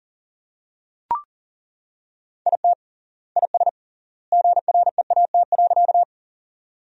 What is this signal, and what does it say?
Morse code, heard as a keyed tone, about 40 wpm, 700 Hz. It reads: IT IS GREAT.
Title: 3x Word-Spacing - Sentences from Top 200 Words - 40wpm